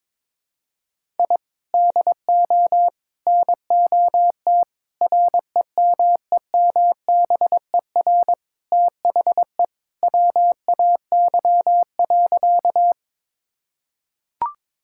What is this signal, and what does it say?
Morse code, heard as a keyed tone, about 22 wpm, 700 Hz. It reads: I DO NOT REMEMBER THE WAY.
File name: practice / 22wpm / Sentences with 4-Word Phrases - 22wpm